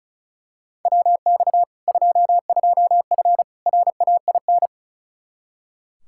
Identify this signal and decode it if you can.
Morse code, heard as a keyed tone, about 36 wpm, 700 Hz. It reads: WX 22F RAIN
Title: Practice QSO - Exchange 2 - Info - 35wpm